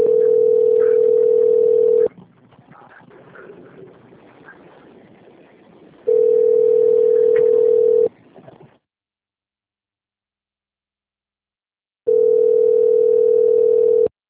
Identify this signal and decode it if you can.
Ringback tone (North America), 440 + 480 Hz.